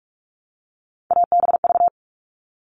Morse code, heard as a keyed tone, about 44 wpm, 700 Hz. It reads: ABV
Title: All Cut Numbers - 3-Digit 45wpm